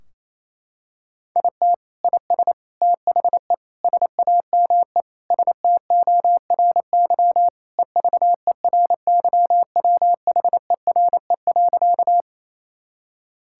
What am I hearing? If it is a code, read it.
Morse code, heard as a keyed tone, about 28 wpm, 700 Hz. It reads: IT IS THE SAME STORY EVERYWHERE.
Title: Sentences from Top 1600 Words 28wpm